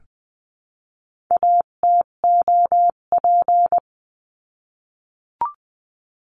Morse code, heard as a keyed tone, about 20 wpm, 700 Hz. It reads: ATOP
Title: 258 Lesson 42c - Words 16-20wpm